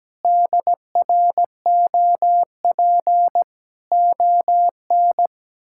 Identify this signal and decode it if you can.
Morse code, heard as a keyed tone, about 17 wpm, 700 Hz. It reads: DROP ON
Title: Pangram Challenge v2 - 17wpm